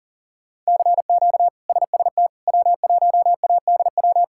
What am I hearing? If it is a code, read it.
Morse code, heard as a keyed tone, about 40 wpm, 700 Hz. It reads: CQ SST W1ADW
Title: K1USN SST QSO Practice - 40wpm